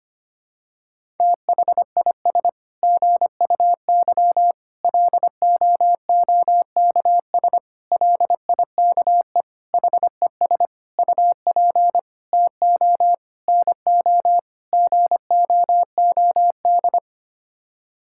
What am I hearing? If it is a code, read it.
Morse code, heard as a keyed tone, about 25 wpm, 700 Hz. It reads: THIS GUY LOOKS LIKE HES UP TO NO GOOD
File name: Sets of 9 Words - 25wpm